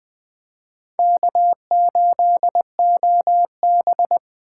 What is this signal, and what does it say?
Morse code, heard as a keyed tone, about 20 wpm, 700 Hz. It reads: K8OB
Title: Speed Racing - USA 1x2 Call Signs 17wpm